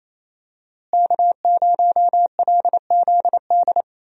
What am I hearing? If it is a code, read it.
Morse code, heard as a keyed tone, about 28 wpm, 700 Hz. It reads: K0LZD
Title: USA Call Signs - 5 Characters - Repeated 3x - 28wpm